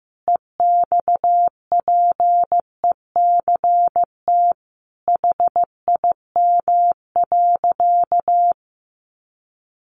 Morse code, heard as a keyed tone, about 15 wpm, 700 Hz. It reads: EXPECT HIM.
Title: Sentences from Top 1100 Words 15wpm